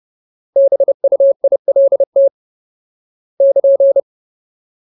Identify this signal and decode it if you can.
Morse code, heard as a keyed tone, about 30 wpm, 550 Hz. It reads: BUILT <KN>
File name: Ultimate Word List / 30WPM Ultimate